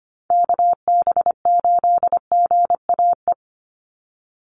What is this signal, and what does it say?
Morse code, heard as a keyed tone, about 25 wpm, 700 Hz. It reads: KB8GAE
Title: Call Signs - 25wpm